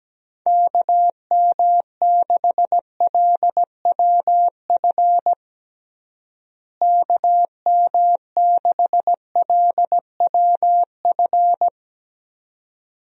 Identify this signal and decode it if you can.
Morse code, heard as a keyed tone, about 17 wpm, 700 Hz. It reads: KM6LWF KM6LWF
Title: USA Call Signs - 6 Characters - Repeated 2x - 17wpm